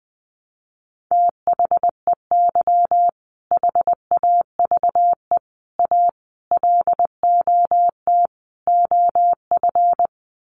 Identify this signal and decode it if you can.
Morse code, heard as a keyed tone, about 20 wpm, 700 Hz. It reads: THEY HAVE A LOT OF
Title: Sets of 5 Words Encore2 20wpm